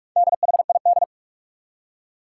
Morse code, heard as a keyed tone, about 46 wpm, 700 Hz. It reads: DHID
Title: Four Challenging Letter-Number Combinations 45wpm